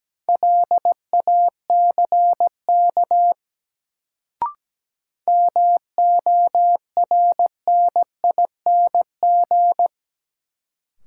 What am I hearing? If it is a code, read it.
Morse code, heard as a keyed tone, about 17 wpm, 700 Hz. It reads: LACK MORNING
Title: Sets of 3 Words - New Word Drill - 17wpm